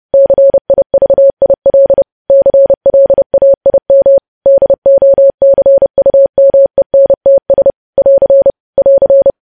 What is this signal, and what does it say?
Morse code, heard as a keyed tone, about 30 wpm, 550 Hz. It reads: CIVIL CLAIM DOCUMENTS <AR> <AR>
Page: news headlines in morse code at 30 wpm